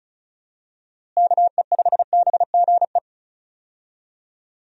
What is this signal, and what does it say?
Morse code, heard as a keyed tone, about 36 wpm, 700 Hz. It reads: KE5BGE
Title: USA Call Signs - 6 Characters - Repeated 2x - 35wpm